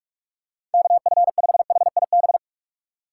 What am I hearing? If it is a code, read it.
Morse code, heard as a keyed tone, about 47 wpm, 700 Hz. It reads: KF5HIB
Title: USA Call Signs - 6 Characters - 3x Character Spacing - 45wpm